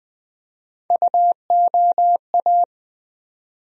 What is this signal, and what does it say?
Morse code, heard as a keyed tone, about 20 wpm, 700 Hz. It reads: UOA
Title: USA Call Sign Postfixes - 3 Letters 20wpm